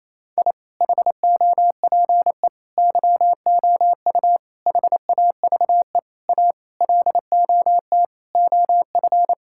Morse code, heard as a keyed tone, about 28 wpm, 700 Hz. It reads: I HOPE YOU HAVE A LOT OF
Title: Sentences from Top 1000 Words - Review Part 6 - 28wpm